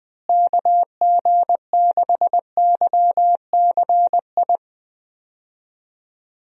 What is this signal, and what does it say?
Morse code, heard as a keyed tone, about 20 wpm, 700 Hz. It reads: KG6YCI
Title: USA Call Signs - 6 Characters - Repeated 3x - 20wpm